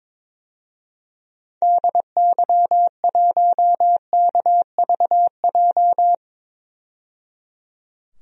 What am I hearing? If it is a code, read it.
Morse code, heard as a keyed tone, about 22 wpm, 700 Hz. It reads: DY1KVJ
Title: DXCC-100 Call Signs - 6 Characters - 22wpm